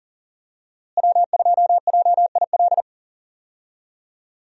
Morse code, heard as a keyed tone, about 38 wpm, 700 Hz. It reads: W2JIL
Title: USA Call Signs - 5 Characters - Repeated 3x - 40wpm